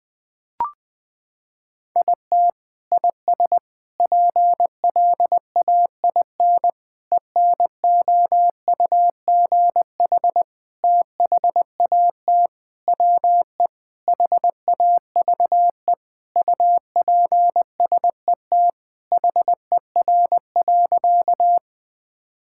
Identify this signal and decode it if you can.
Morse code, heard as a keyed tone, about 20 wpm, 700 Hz. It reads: IT IS PLAIN ENOUGH THAT WE HAVE UPSET HER.
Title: Sentences from Top 1000 Words - Review Part 5 - 20wpm